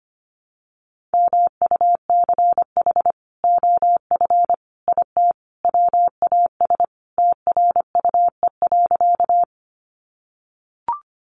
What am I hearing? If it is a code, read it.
Morse code, heard as a keyed tone, about 25 wpm, 700 Hz. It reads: MUCH OF IT WAS TRUE.
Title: Sentences from Top 400 Words 25wpm